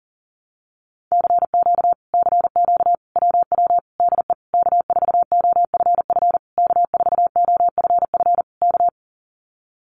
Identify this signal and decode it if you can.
Morse code, heard as a keyed tone, about 43 wpm, 700 Hz. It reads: CQ CQ WW DE K4OFF K4OFF K
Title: Common QSO Phrases Encore 40wpm